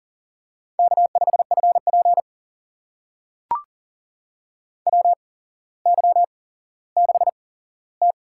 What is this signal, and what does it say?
Morse code, heard as a keyed tone, about 40 wpm, spaced as for 17 wpm, 700 Hz. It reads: K5FP WY6T
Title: USA Call Signs - 4 Characters - 8x Character Spacing - 40wpm